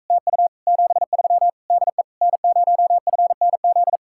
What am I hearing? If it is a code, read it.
Morse code, heard as a keyed tone, about 45 wpm, 700 Hz. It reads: TU 73 DE N0FNZ